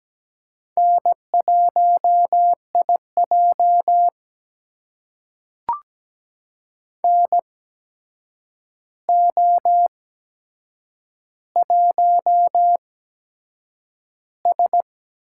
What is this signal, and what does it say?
Morse code, heard as a keyed tone, about 17 wpm, spaced as for 7 wpm, 700 Hz. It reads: N1IJ NO1S
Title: USA Call Signs - 4 Characters - 8x Character Spacing - 17wpm